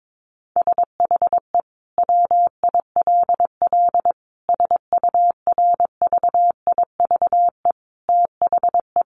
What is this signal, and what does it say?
Morse code, heard as a keyed tone, about 22 wpm, 700 Hz. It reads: SHE WILL SURVIVE THE
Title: Sentences from Top 1300 Words 22wpm